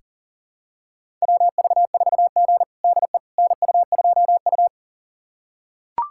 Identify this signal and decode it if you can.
Morse code, heard as a keyed tone, about 40 wpm, 700 Hz. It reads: WV4G DE NU2U
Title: USA Callsign DE Callsign - 4 Characters - 40wpm